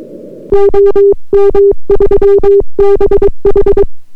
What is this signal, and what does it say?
Morse code, heard as a keyed tone, about 22 wpm, 400 Hz. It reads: OM3BH